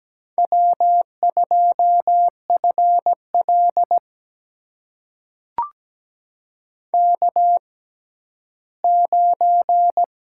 Morse code, heard as a keyed tone, about 17 wpm, spaced as for 11 wpm, 700 Hz. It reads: W2FL K9
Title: USA Call Signs - 4 Characters - 6x Character Spacing - 17wpm